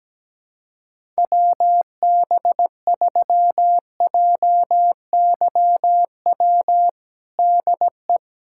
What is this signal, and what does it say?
Morse code, heard as a keyed tone, about 17 wpm, 700 Hz. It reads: WB3JYW DE
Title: USA Callsign DE Callsign - 6 Characters - 17wpm